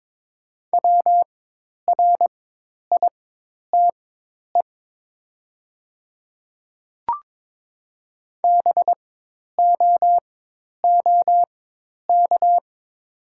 Morse code, heard as a keyed tone, about 22 wpm, spaced as for 10 wpm, 700 Hz. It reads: WRITE BOOK
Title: 4x Character Spacing - Top 300 Words - 22wpm